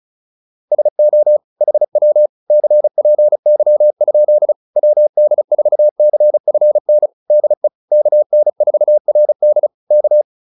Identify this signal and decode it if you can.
Morse code, heard as a keyed tone, about 34 wpm, 600 Hz. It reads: SO HW CPY? WD4CFN DE KN4RD K
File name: QSO207 35